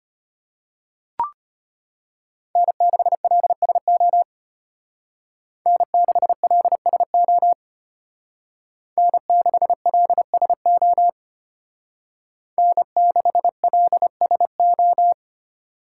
Morse code, heard as a keyed tone, about 31 wpm, 700 Hz. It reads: N6LSO N6LSO N6LSO N6LSO